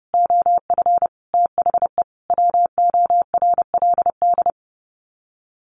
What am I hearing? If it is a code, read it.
Morse code, heard as a keyed tone, about 30 wpm, 700 Hz. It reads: OF THE WORLD
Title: Sets of 5 Words Encore2 30wpm